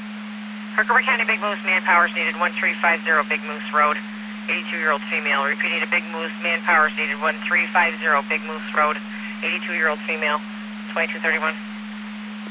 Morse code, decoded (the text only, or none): none